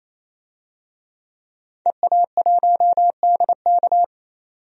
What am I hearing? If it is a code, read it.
Morse code, heard as a keyed tone, about 28 wpm, 700 Hz. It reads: EA1DK
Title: DXCC-100 Call Signs - 5 Characters - 28wpm